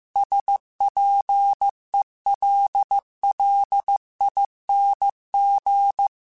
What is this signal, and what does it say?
Morse code, heard as a keyed tone, about 15 wpm, 800 Hz. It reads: SPELLING